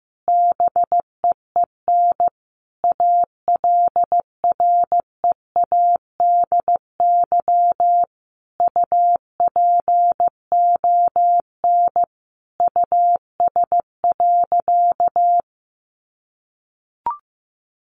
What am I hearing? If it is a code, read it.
Morse code, heard as a keyed tone, about 15 wpm, 700 Hz. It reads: BEEN ALREADY UPON US.
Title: Sentences from Top 1500 Words 15wpm